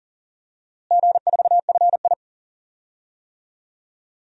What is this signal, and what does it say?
Morse code, heard as a keyed tone, about 40 wpm, 700 Hz. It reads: G4FI
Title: DXCC-100 Call Signs - 4 Characters - Repeated 3x - 40wpm